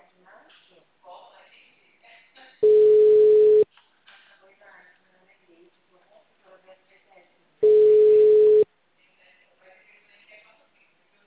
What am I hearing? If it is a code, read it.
Ringback tone (Europe, 425 Hz), with speech.